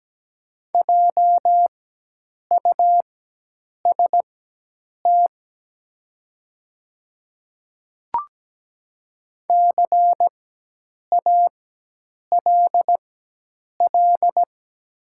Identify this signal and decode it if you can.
Morse code, heard as a keyed tone, about 17 wpm, spaced as for 8 wpm, 700 Hz. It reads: JUST CALL
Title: 4x Character Spacing - Top 300 Words - 17wpm